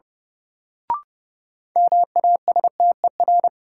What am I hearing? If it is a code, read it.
Morse code, heard as a keyed tone, about 30 wpm, 700 Hz. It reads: MASTER
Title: Sentences from Top 1300 Words - New Word Drill - 30wpm